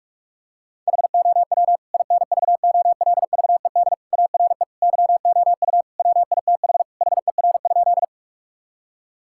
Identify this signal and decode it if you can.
Morse code, heard as a keyed tone, about 46 wpm, 700 Hz. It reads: HOW INVOLVED ARE YOU WITH HER?
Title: Sentences from Top 1000 Words - Review Part 5 - 45wpm